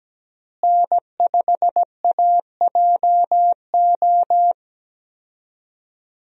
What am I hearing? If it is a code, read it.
Morse code, heard as a keyed tone, about 17 wpm, 700 Hz. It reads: N5AJO
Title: USA Call Signs - 5 Characters - Repeated 3x - 17wpm